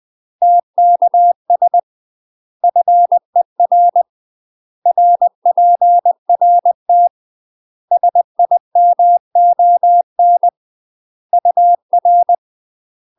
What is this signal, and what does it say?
Morse code, heard as a keyed tone, about 20 wpm, 700 Hz. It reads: TKS FER RPRT SIMON UR
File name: UK MASTER QSOs TRAINING / 20WPM / UK MASTER QSOs 0043